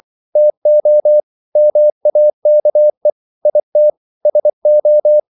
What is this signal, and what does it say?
Morse code, heard as a keyed tone, about 24 wpm, 600 Hz. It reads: TO MAKE IT SO
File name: CW / Practice / EnglishQuotations0224